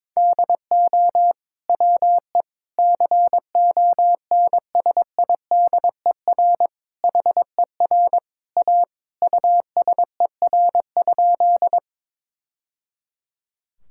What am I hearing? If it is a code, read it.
Morse code, heard as a keyed tone, about 22 wpm, 700 Hz. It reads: DO WE CONSIDER HER A USER?
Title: Sentences from Top 1000 Words - Review Part 5 - 22wpm